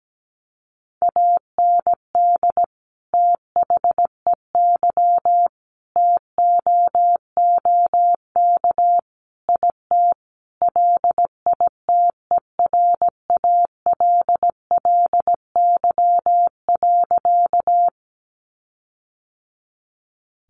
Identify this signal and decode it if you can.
Morse code, heard as a keyed tone, about 17 wpm, 700 Hz. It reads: AND THEY TOOK IT LITERALLY.
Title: Sentences from Top 1400 Words 17wpm